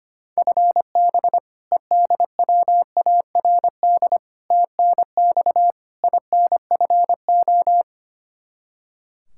Morse code, heard as a keyed tone, about 25 wpm, 700 Hz. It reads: FB EDWARD TNX INFO
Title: Practice QSO - Exchange 3 - More Info - 25wpm